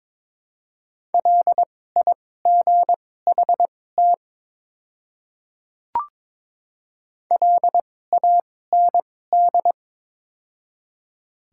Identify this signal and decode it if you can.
Morse code, heard as a keyed tone, about 22 wpm, spaced as for 16 wpm, 700 Hz. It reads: LIGHT LAND
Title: 2x Character Spacing - Top 200 Words - 22wpm